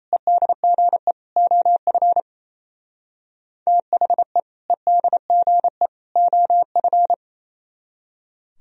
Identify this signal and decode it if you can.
Morse code, heard as a keyed tone, about 32 wpm, 700 Hz. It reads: EDGE OF THE EDGE OF
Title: Speed-Racing - Sets of 3 Words 28wpm